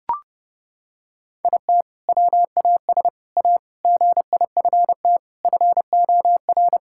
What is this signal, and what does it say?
Morse code, heard as a keyed tone, about 32 wpm, 700 Hz. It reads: IT WAS A GIFT FOR
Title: Sentences from Top 1000 Words - Review Part 6 - 30wpm